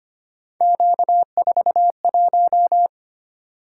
Morse code, heard as a keyed tone, about 25 wpm, 700 Hz. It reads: Q41
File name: Three Letter-Number Combinations - 25wpm